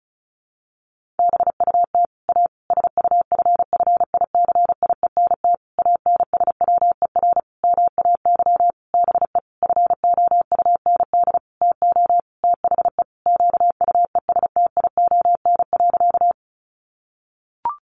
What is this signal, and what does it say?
Morse code, heard as a keyed tone, about 34 wpm, 700 Hz. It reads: BUT A SUFFICIENT ANSWER MAY BE FOUND TO THE QUESTION.
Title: Sentences from Top 1400 Words 35wpm